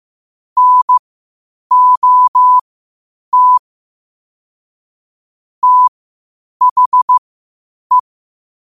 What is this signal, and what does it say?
Morse code, heard as a keyed tone, about 15 wpm, spaced as for 8 wpm, 1000 Hz. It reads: NOT THE
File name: qotdfiles / 8wpm